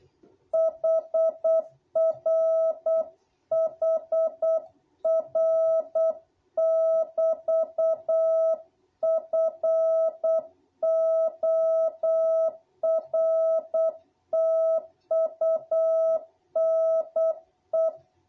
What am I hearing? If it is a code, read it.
Morse code, heard as a keyed tone, about 8 wpm, 660 Hz. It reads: HR HR<BT>FORTUNE
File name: jh8cbh / 2021cw